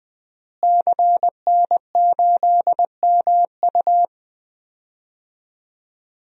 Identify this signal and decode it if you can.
Morse code, heard as a keyed tone, about 21 wpm, 700 Hz. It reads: CN8MU
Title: DXCC-100 Call Signs - 5 Characters - Repeated 3x - 20wpm